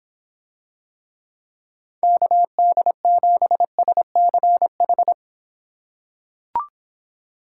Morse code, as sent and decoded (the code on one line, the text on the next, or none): -.- -.. --... ... -.-. ....
KD7SCH